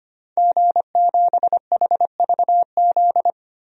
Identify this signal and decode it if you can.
Morse code, heard as a keyed tone, about 25 wpm, 700 Hz. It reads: G7HVZ
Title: DXCC-100 Call Signs - 5 Characters - 25wpm